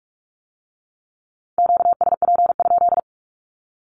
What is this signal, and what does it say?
Morse code, heard as a keyed tone, about 46 wpm, 700 Hz. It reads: QSP?
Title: Q-Codes with Spoken Meaning - 45wpm